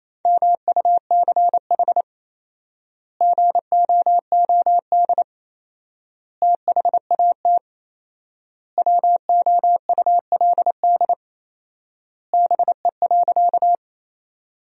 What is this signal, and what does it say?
Morse code, heard as a keyed tone, about 28 wpm, 700 Hz. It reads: MUCH GOOD THAT WOULD BE.
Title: 4x Word-Spacing - Sentences from Top 200 Words - 28wpm